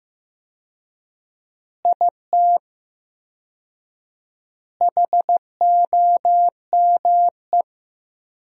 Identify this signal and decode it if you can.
Morse code, heard as a keyed tone, about 14 wpm, 700 Hz. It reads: IT HOME
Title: 4x Word-Spacing - Sentences from Top 500 Words - 15wpm